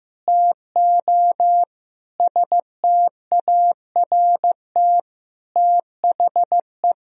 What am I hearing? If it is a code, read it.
Morse code, heard as a keyed tone, about 15 wpm, 700 Hz. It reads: TO START THE
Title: Sentences from Top 1000 Words - Review Part 1 - 15wpm